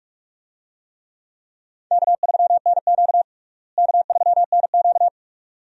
Morse code, heard as a keyed tone, about 46 wpm, 700 Hz. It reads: K3NQ K3NQ
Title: USA Call Signs - 4 Characters - Repeated 2x - 45wpm